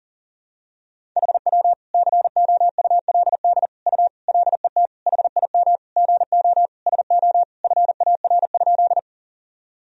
Morse code, heard as a keyed tone, about 37 wpm, 700 Hz. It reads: HW COULD U LET HIM GO SO FAR?
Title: Sentences from Top 300 Words and Abbreviations 40wpm